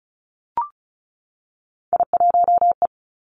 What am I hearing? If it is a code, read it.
Morse code, heard as a keyed tone, about 36 wpm, 700 Hz. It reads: I1E
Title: Three Challenging Letter-Number Combinations 35wpm